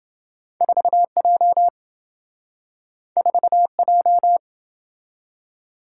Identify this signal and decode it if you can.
Morse code, heard as a keyed tone, about 28 wpm, 700 Hz. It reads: 4J 4J